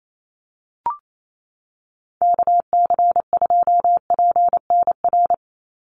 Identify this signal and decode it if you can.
Morse code, heard as a keyed tone, about 28 wpm, 700 Hz. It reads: KC2PNR
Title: USA Call Signs 28wpm